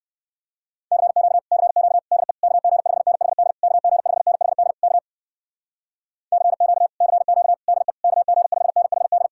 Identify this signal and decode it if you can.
Morse code, heard as a keyed tone, about 58 wpm, 700 Hz. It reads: CQ CQ DE KG5NHD KG5NHD K CQ CQ DE KG5NHD